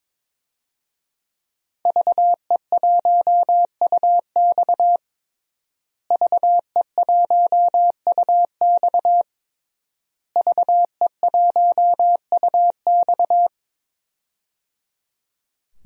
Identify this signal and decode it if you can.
Morse code, heard as a keyed tone, about 22 wpm, 700 Hz. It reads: VE1UX VE1UX VE1UX
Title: DXCC-100 Call Signs - 5 Characters - Repeated 3x - 22wpm